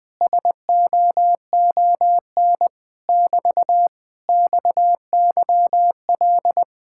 Morse code, heard as a keyed tone, about 20 wpm, 700 Hz. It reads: SOON <BT> XYL
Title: Speed-Racing Common QSO Phrases Encore - 20wpm